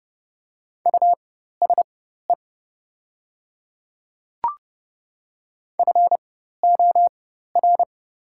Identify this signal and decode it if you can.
Morse code, heard as a keyed tone, about 30 wpm, spaced as for 14 wpm, 700 Hz. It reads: USE FOR
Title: 4x Character Spacing - Top 100 Words - 30wpm